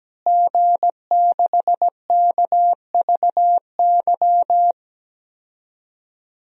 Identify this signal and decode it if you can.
Morse code, heard as a keyed tone, about 17 wpm, 700 Hz. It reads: G6KVY
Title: DXCC-100 Call Signs - 5 Characters - Repeated 2x - 17wpm